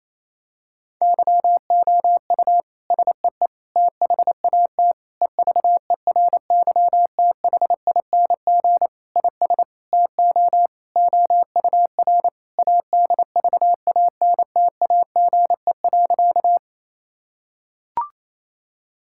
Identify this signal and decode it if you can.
Morse code, heard as a keyed tone, about 27 wpm, 700 Hz. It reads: YOU SEE THAT EVERYTHING IS TO OUR ADVANTAGE.
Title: Sentences from Top 1000 Words - Review Part 6 - 28wpm